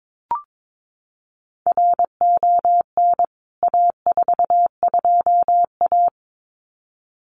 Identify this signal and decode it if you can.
Morse code, heard as a keyed tone, about 22 wpm, 700 Hz. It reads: RON A42A
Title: CWT Exchange 22wpm